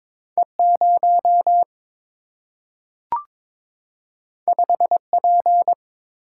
Morse code, heard as a keyed tone, about 22 wpm, 700 Hz. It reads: E 0 5 P
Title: Single-Letter Single-Number 22wpm